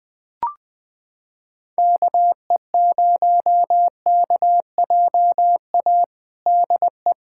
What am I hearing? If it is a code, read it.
Morse code, heard as a keyed tone, about 20 wpm, 700 Hz. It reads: KE0KJA DE
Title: USA Callsign DE Callsign - 6 Characters - 20wpm